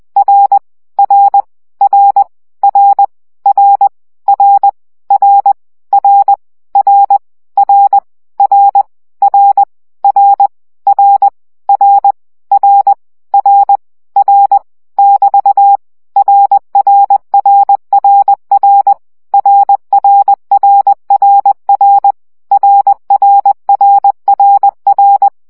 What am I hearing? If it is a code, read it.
Morse code, heard as a keyed tone, about 20 wpm, 800 Hz. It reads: R R R R R R R R R R R R R R R R R R <BT> RRRRR RRRRR RRRRR
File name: CW gyakorlas r 17 baud